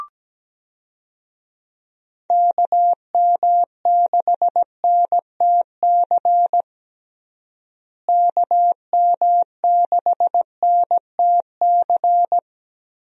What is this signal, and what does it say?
Morse code, heard as a keyed tone, about 17 wpm, 700 Hz. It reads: KM6NTC KM6NTC
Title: USA Call Signs - 6 Characters - Repeated 3x - 17wpm